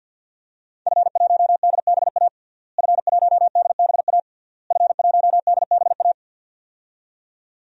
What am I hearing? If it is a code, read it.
Morse code, heard as a keyed tone, about 52 wpm, 700 Hz. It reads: F1DBA F1DBA F1DBA
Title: DXCC-100 Call Signs - 5 Characters - Repeated 3x - 50wpm